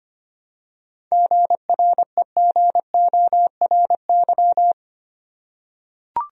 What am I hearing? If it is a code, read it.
Morse code, heard as a keyed tone, about 25 wpm, 700 Hz. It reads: GREGORY
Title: USA Common Names - 25wpm